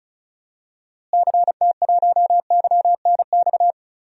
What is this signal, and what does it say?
Morse code, heard as a keyed tone, about 36 wpm, 700 Hz. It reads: CT1YNX
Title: DXCC-100 Call Signs - 6 Characters - Repeated 3x - 35wpm